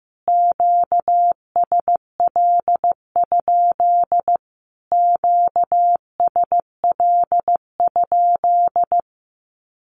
Morse code, heard as a keyed tone, about 15 wpm, 700 Hz. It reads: QSL? QSL?
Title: Q-Codes Repeated 3x 15wpm